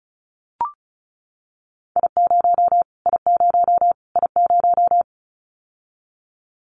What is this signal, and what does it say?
Morse code, heard as a keyed tone, about 34 wpm, 700 Hz. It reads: I0 I0 I0
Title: Callsign Prefixes for Top 25 DXCC - Repeated 3x - 35wpm